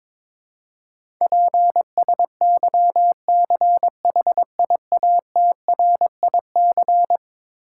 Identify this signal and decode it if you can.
Morse code, heard as a keyed tone, about 22 wpm, 700 Hz. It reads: PSYCHIATRIC
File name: practice / 22wpm / Eleven Letter Words - 22wpm